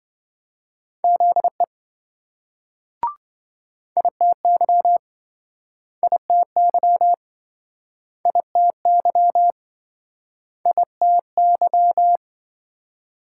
Morse code, heard as a keyed tone, about 28 wpm, 700 Hz. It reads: ZE ITY ITY ITY ITY